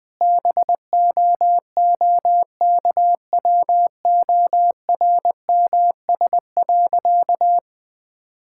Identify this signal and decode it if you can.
Morse code, heard as a keyed tone, about 20 wpm, 700 Hz. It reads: BOOKWORMS.
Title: All-Age Appropriate Jokes - 20wpm